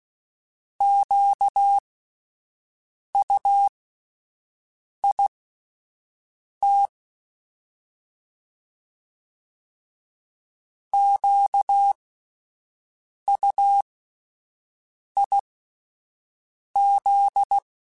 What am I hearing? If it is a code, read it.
Morse code, heard as a keyed tone, about 16 wpm, spaced as for 5 wpm, 800 Hz. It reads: QUIT QUIZ
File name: Alphabetical / Alphabetical mp3s / Qs @5